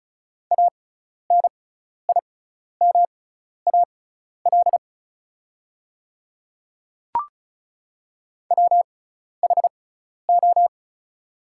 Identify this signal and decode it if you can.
Morse code, heard as a keyed tone, about 36 wpm, spaced as for 12 wpm, 700 Hz. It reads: ANIMAL WHO
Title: 6x Character Spacing - Top 300 Words - 35wpm